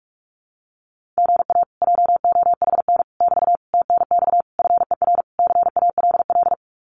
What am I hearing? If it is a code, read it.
Morse code, heard as a keyed tone, about 44 wpm, 700 Hz. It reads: GA JOHN <BT> TNX FER CALL
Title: Common QSO Phrases Encore 45wpm